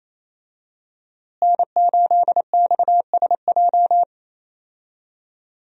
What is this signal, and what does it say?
Morse code, heard as a keyed tone, about 28 wpm, 700 Hz. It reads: N8XSJ
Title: USA Call Signs - 5 Characters - Repeated 3x - 28wpm